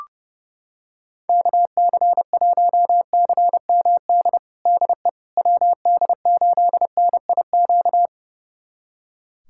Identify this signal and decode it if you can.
Morse code, heard as a keyed tone, about 30 wpm, 700 Hz. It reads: KC1CMD DE WD8NIQ
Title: USA Callsign DE Callsign 30wpm